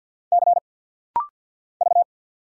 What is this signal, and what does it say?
Morse code, heard as a keyed tone, about 50 wpm, 700 Hz. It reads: C V
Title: Single Letter-Number Alt - Rapid-Fire 50wpm